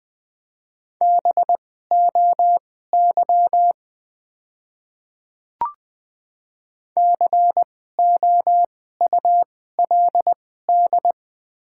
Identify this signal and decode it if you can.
Morse code, heard as a keyed tone, about 20 wpm, spaced as for 15 wpm, 700 Hz. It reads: BOY COULD
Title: 2x Character Spacing - Top 200 Words - 20wpm